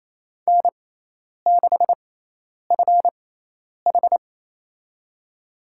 Morse code, heard as a keyed tone, about 27 wpm, spaced as for 10 wpm, 700 Hz. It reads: N6FH
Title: USA Call Signs - 4 Characters - 6x Character Spacing - 28wpm